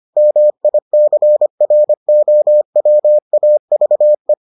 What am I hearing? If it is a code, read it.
Morse code, heard as a keyed tone, about 25 wpm, 600 Hz. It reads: MICROWAVE